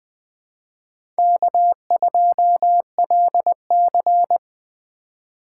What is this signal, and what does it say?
Morse code, heard as a keyed tone, about 21 wpm, 700 Hz. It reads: K2LC
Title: USA Call Signs - 4 Characters - 4x Character Spacing - 20wpm